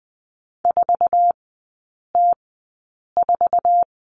Morse code, heard as a keyed tone, about 20 wpm, spaced as for 8 wpm, 700 Hz. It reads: 4T4
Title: 053 Lesson 10e - Letter r - Three Characters 20wpm